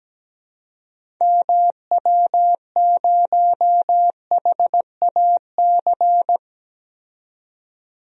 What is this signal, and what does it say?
Morse code, heard as a keyed tone, about 17 wpm, 700 Hz. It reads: MW0HAC
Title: DXCC-100 Call Signs - 6 Characters - 17wpm